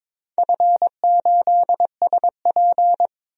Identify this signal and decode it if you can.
Morse code, heard as a keyed tone, about 22 wpm, 700 Hz. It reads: F8SP
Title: DXCC-100 Call Signs - 4 Characters - 22wpm